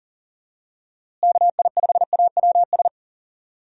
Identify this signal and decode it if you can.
Morse code, heard as a keyed tone, about 40 wpm, 700 Hz. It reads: KI5AWS
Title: USA Call Signs - 6 Characters - Repeated 2x - 40wpm